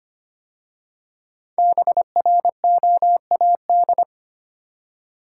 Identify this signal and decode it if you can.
Morse code, heard as a keyed tone, about 25 wpm, 700 Hz. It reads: BROAD
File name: 5 Letter Words - First 3 Letters Given - 25wpm